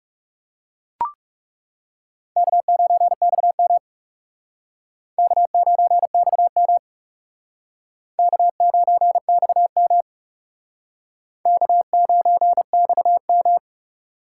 Morse code, heard as a keyed tone, about 37 wpm, 700 Hz. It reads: K9XM K9XM K9XM K9XM
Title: Speed Racing - USA 1x2 Call Signs 30wpm